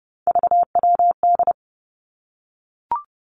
Morse code, heard as a keyed tone, about 32 wpm, 700 Hz. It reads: VWD
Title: Three Challenging Letter-Number Combinations 30wpm